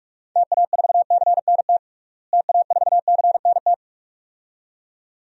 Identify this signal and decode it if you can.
Morse code, heard as a keyed tone, about 45 wpm, 690 Hz. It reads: TA4CNT TA4CNT
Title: DXCC-100 Call Signs - 6 Characters - Repeated 2x - 45wpm